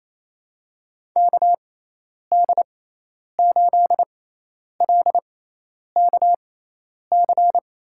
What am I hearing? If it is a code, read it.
Morse code, heard as a keyed tone, about 28 wpm, spaced as for 10 wpm, 700 Hz. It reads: KD8LKC
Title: USA Call Signs - 6 Characters - 6x Character Spacing - 28wpm